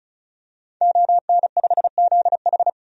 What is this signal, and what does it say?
Morse code, heard as a keyed tone, about 36 wpm, 700 Hz. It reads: ON5ZH
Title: DXCC-100 Call Signs - 5 Characters - 35wpm